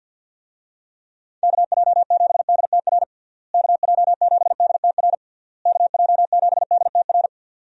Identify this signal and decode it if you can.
Morse code, heard as a keyed tone, about 52 wpm, 700 Hz. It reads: KJ7DTR KJ7DTR KJ7DTR
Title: USA Call Signs - 6 Characters - Repeated 3x - 50wpm